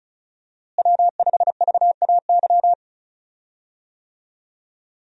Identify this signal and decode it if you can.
Morse code, heard as a keyed tone, about 36 wpm, 700 Hz. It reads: W5VAY
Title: USA Call Signs - 5 Characters - Repeated 2x - 35wpm